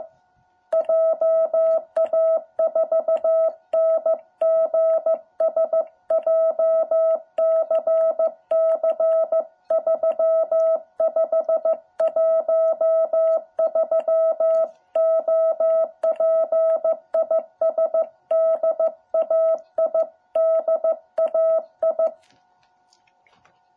Morse code, heard as a keyed tone, about 15 wpm, 660 Hz. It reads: JA4NGS JCC3513 OP IS DAI DAI
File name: jh8cbh / 2021cw